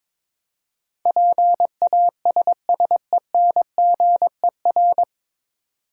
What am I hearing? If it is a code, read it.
Morse code, heard as a keyed tone, about 22 wpm, 690 Hz. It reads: PASSENGER
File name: 9 Letter Words - First 3 Letters Given - 22wpm